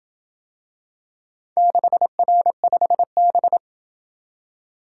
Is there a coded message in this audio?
Yes: Morse code from 1.6 to 3.6 s, heard as a keyed tone, about 27 wpm, 700 Hz. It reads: 6R5B